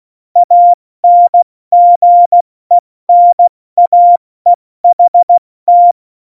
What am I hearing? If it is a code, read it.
Morse code, heard as a keyed tone, about 16 wpm, 700 Hz. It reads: ANGENAEHT
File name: Morsekurs / Witze / Witz 01. Dezember 2020, 70 F80